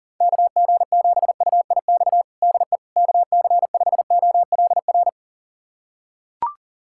Morse code, heard as a keyed tone, about 37 wpm, 700 Hz. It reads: KG7UIX DE KC5OLR
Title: USA Callsign DE Callsign - 6 Characters - 40wpm